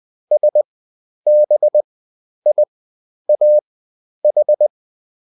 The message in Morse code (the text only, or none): SBIAH